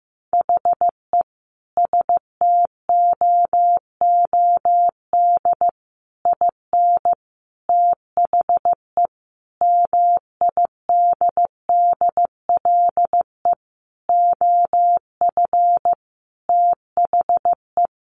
Morse code, heard as a keyed tone, about 14 wpm, 700 Hz. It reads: HE STOOD IN THE MIDDLE OF THE
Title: Sentences from Top 1400 Words 15wpm